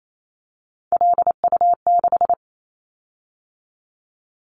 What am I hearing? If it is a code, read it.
Morse code, heard as a keyed tone, about 28 wpm, 700 Hz. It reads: LU6
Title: Callsign Prefixes for Top 100 DXCC - 28wpm